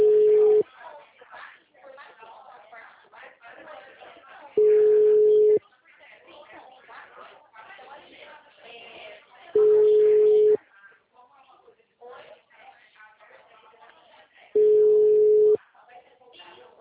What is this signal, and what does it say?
Ringback tone (Europe, 425 Hz), with speech.
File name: out- -2029